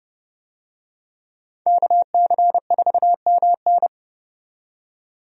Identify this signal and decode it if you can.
Morse code, heard as a keyed tone, about 30 wpm, 700 Hz. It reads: KC4MN
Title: USA Call Signs - 5 Characters - 30wpm